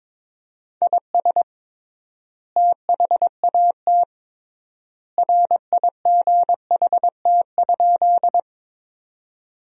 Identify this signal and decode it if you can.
Morse code, heard as a keyed tone, about 21 wpm, 700 Hz. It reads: IS THAT RIGHT?
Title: 3x Word-Spacing - Sentences from Top 200 Words - 22wpm